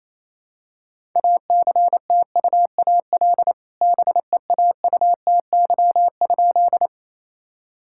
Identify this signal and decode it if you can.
Morse code, heard as a keyed tone, about 28 wpm, 700 Hz. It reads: ACTUAL BEAUTY?
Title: Sentences from Top 1000 Words - Review Part 5 - 28wpm